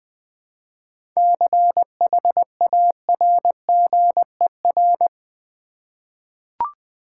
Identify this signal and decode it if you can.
Morse code, heard as a keyed tone, about 20 wpm, 700 Hz. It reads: CHARGER
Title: Easy Words with Hints - 20wpm